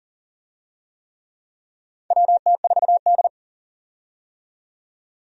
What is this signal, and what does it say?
Morse code, heard as a keyed tone, about 40 wpm, 700 Hz. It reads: WT4D
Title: USA Call Signs - 4 Characters - 40wpm